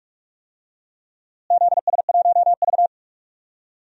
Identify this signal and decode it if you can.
Morse code, heard as a keyed tone, about 45 wpm, 700 Hz. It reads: ZS1V
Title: DXCC-100 Call Signs - 4 Characters - Repeated 2x - 45wpm